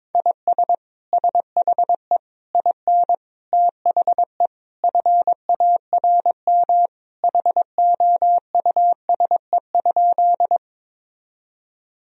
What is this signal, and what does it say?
Morse code, heard as a keyed tone, about 22 wpm, 700 Hz. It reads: IS SHE IN THE FARM HOUSE?
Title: Sentences from Top 1000 Words - Review Part 2 - 22wpm